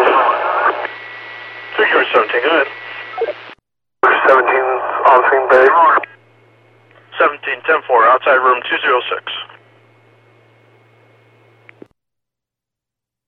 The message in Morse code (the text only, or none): none